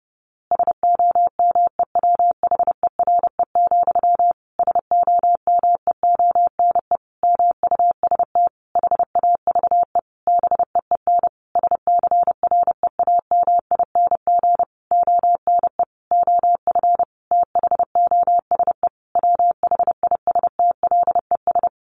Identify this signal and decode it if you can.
Morse code, heard as a keyed tone, about 28 wpm, 700 Hz. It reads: SOMEWHERE, SOMEONE MUST HAVE BEEN SCREAMING ONE OF THOSE WHISTLES
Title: A Sound Of Thunder 30wpm